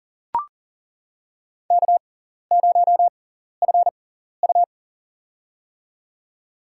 Morse code, heard as a keyed tone, about 37 wpm, spaced as for 13 wpm, 700 Hz. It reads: K0FU
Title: USA Call Signs - 6x Character Spacing - 40wpm